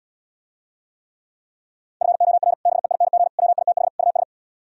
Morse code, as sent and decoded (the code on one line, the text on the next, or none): ..-. --- .-. / -... .. -. --. / .-- .. - .... / ..- ...
FOR BEING WITH US